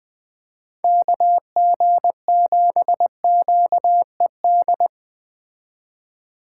Morse code, heard as a keyed tone, about 20 wpm, 700 Hz. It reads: KG7QED